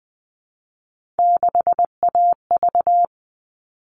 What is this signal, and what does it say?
Morse code, heard as a keyed tone, about 20 wpm, 700 Hz. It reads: 6AV
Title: All Cut Numbers - 3-Digit 20wpm